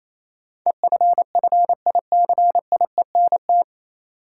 Morse code, heard as a keyed tone, about 28 wpm, 700 Hz. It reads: EFFICIENT
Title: Sentences from Top 1300 Words - New Word Drill - 28wpm